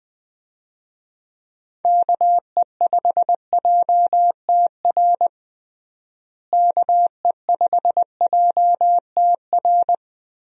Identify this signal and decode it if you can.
Morse code, heard as a keyed tone, about 21 wpm, 700 Hz. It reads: KE5JTR KE5JTR
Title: USA Call Signs - 6 Characters - Repeated 3x - 20wpm